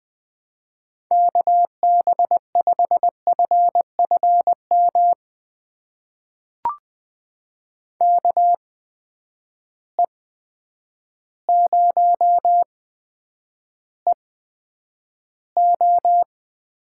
Morse code, heard as a keyed tone, about 20 wpm, spaced as for 9 wpm, 700 Hz. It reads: KB5FFM KE0EO
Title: USA Call Signs - 6 Characters - 8x Character Spacing - 20wpm